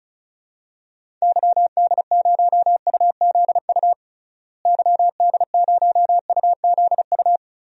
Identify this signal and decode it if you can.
Morse code, heard as a keyed tone, about 34 wpm, 700 Hz. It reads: YD0UZU YD0UZU
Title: DXCC-100 Call Signs - 6 Characters - Repeated 3x - 35wpm